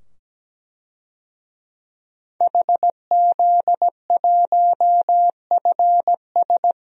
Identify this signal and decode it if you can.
Morse code, heard as a keyed tone, about 17 wpm, 700 Hz. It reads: HZ1FS
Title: DXCC-100 Call Signs - 5 Characters - 17wpm